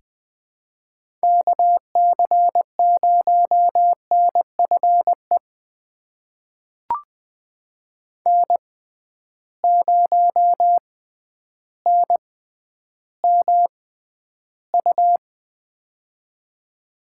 Morse code, heard as a keyed tone, about 20 wpm, spaced as for 11 wpm, 700 Hz. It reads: KC0NFE N0NMU
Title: USA Call Signs - 6x Character Spacing - 20wpm